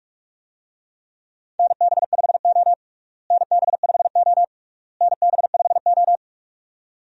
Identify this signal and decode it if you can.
Morse code, heard as a keyed tone, about 46 wpm, 700 Hz. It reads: NB5O NB5O NB5O
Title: USA Call Signs - 4 Characters - Repeated 3x - 45wpm